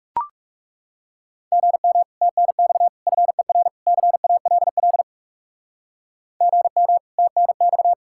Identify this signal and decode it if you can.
Morse code, heard as a keyed tone, about 40 wpm, 700 Hz. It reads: GM TNX FER CALL GM TNX